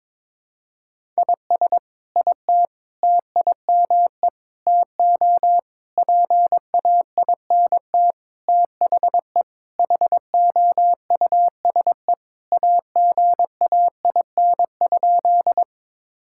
Morse code, heard as a keyed tone, about 22 wpm, 700 Hz. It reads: IS IT TIME TO PAINT THE HOUSE AGAIN?
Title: Sentences from Top 1000 Words - Review Part 6 - 22wpm